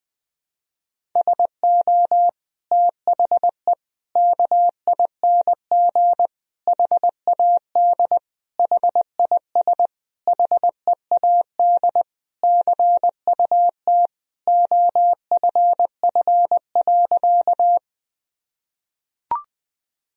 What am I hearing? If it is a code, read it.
Morse code, heard as a keyed tone, about 20 wpm, 700 Hz. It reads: SO THE KING HAD HIS HEAD CUT OFF.
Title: Sentences from Top 1000 Words - Review Part 1 - 20wpm